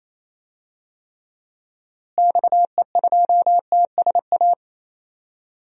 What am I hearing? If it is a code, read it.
Morse code, heard as a keyed tone, about 28 wpm, 700 Hz. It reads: XE2TSA
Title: DXCC-100 Call Signs - 6 Characters - Repeated 2x - 28wpm